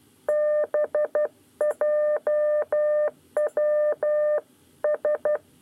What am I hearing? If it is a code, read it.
Morse code, heard as a keyed tone, about 12 wpm, 600 Hz. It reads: BJW S
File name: 9 lezione cw 2